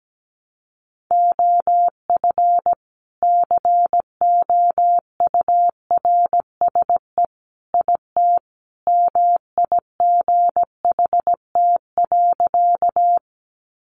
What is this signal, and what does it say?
Morse code, heard as a keyed tone, about 17 wpm, 700 Hz. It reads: OF COURSE IT MIGHT.
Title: Sentences from Top 500 Words 17wpm